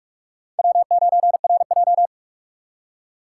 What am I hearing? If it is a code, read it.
Morse code, heard as a keyed tone, about 46 wpm, 700 Hz. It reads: W9RJ
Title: USA Call Signs - 4 Characters - 4x Character Spacing - 45wpm